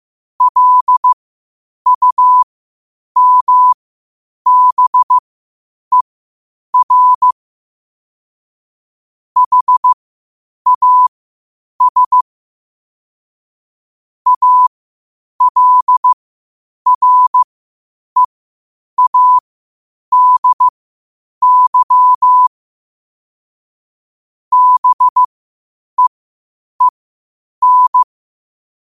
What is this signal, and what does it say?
Morse code, heard as a keyed tone, about 15 wpm, spaced as for 8 wpm, 1000 Hz. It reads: LUMBER HAS ALREADY BEEN
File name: qotdfiles / 8wpm